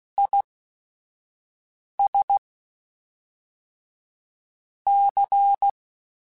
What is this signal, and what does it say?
Morse code, heard as a keyed tone, about 16 wpm, spaced as for 5 wpm, 800 Hz. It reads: IS C
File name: Technical Subjects / Technical Subjects-mp3s / XL @5